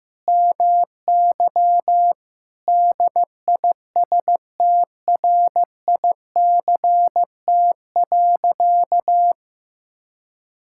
Morse code, heard as a keyed tone, about 15 wpm, 700 Hz. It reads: MY DISTRICT.
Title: Sentences from Top 1600 Words 15wpm